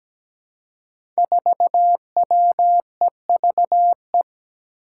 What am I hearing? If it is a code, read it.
Morse code, heard as a keyed tone, about 17 wpm, 700 Hz. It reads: 4WEVE
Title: Five Challenging Letter-Number Combinations 17wpm